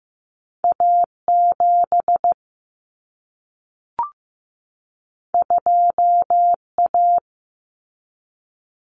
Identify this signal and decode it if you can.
Morse code, heard as a keyed tone, about 15 wpm, 700 Hz. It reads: A 7 2 A
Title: Common Cut Numbers - 2-Digit 15wpm